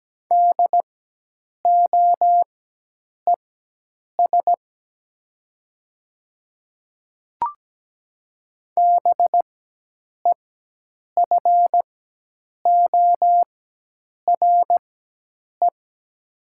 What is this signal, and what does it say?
Morse code, heard as a keyed tone, about 17 wpm, spaced as for 8 wpm, 700 Hz. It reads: DOES BEFORE
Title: 4x Character Spacing - Top 300 Words - 17wpm